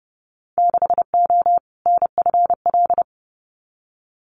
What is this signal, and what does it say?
Morse code, heard as a keyed tone, about 30 wpm, 700 Hz. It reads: 6O NFL
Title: Winter Field Day Exchanges 30wpm